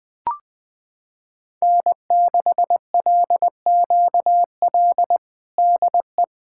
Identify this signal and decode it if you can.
Morse code, heard as a keyed tone, about 20 wpm, 700 Hz. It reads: N6LQL DE
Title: USA Callsign DE Callsign - 5 Characters - 20wpm